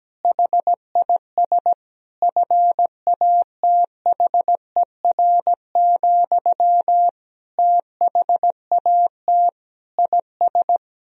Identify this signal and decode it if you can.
Morse code, heard as a keyed tone, about 17 wpm, 700 Hz. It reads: HIS FATHER, THAT IS
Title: Sentences from Top 1000 Words - Review Part 4 - 17wpm